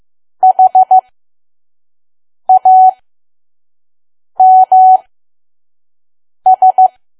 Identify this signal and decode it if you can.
Morse code, heard as a keyed tone, about 15 wpm, spaced as for 5 wpm, 750 Hz. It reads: HAMS
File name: Archive / 5 WPM files / PM